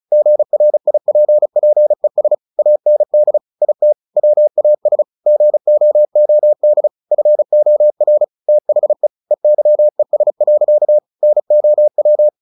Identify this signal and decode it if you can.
Morse code, heard as a keyed tone, about 34 wpm, 600 Hz. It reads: GRIPPES AND IT WAS GOOD FOR THE EYES. NOW
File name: the old man and the sea / the old man and the sea 1 35wpm